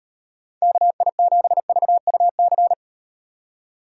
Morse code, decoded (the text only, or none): KI7VUC